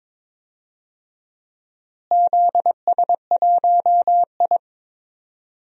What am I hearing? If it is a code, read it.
Morse code, heard as a keyed tone, about 22 wpm, 700 Hz. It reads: ZS1I
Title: DXCC-100 Call Signs - 4 Characters - Repeated 3x - 22wpm